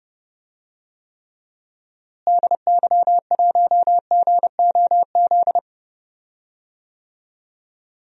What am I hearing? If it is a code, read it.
Morse code, heard as a keyed tone, about 31 wpm, 700 Hz. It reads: DY1GOZ
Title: DXCC-100 Call Signs - 6 Characters - Repeated 2x - 30wpm